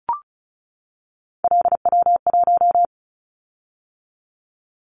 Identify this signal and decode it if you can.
Morse code, heard as a keyed tone, about 36 wpm, 700 Hz. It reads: LW1
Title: Callsign Prefixes for Top 50 DXCC - 35wpm